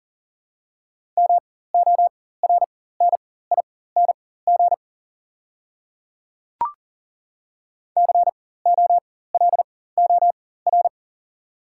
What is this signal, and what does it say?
Morse code, heard as a keyed tone, about 40 wpm, spaced as for 19 wpm, 690 Hz. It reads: MORNING COLOR
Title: 4x Character Spacing - Top 500 Words - 40wpm